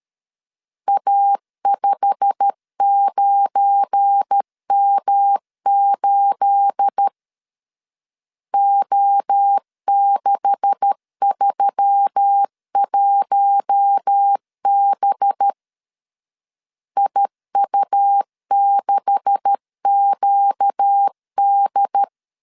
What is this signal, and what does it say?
Morse code, heard as a keyed tone, about 13 wpm, 800 Hz. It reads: A59M8 O631B IU6QD
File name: CW / DET / 1 9 1500C S C 12wpm 3811